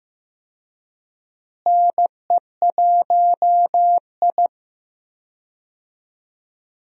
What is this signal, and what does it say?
Morse code, heard as a keyed tone, about 15 wpm, 700 Hz. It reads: NE1I
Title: USA Call Signs - 4 Characters - Repeated 2x - 15wpm